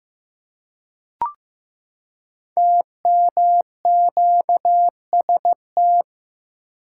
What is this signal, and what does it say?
Morse code, heard as a keyed tone, about 15 wpm, 700 Hz. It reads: TMQST
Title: Code Groups - 5 Letter:Numbers - 15wpm